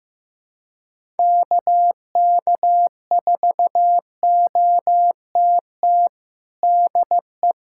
Morse code, heard as a keyed tone, about 15 wpm, 700 Hz. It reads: KK4OTT DE
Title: USA Callsign DE Callsign 15wpm